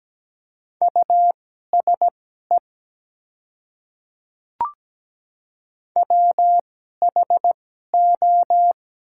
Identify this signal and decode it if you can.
Morse code, heard as a keyed tone, about 17 wpm, spaced as for 12 wpm, 700 Hz. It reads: USE WHO
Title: 2x Character Spacing - Top 100 Words - 17wpm